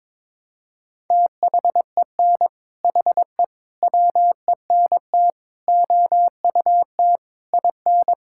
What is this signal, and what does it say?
Morse code, heard as a keyed tone, about 22 wpm, 700 Hz. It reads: THEN HE WENT OUT IN
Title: Sentences from Top 1000 Words - Review Part 5 - 22wpm